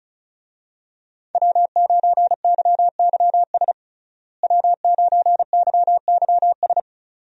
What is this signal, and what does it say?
Morse code, heard as a keyed tone, about 36 wpm, 700 Hz. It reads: W9YYS W9YYS
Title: USA Call Signs - 5 Characters - Repeated 2x - 35wpm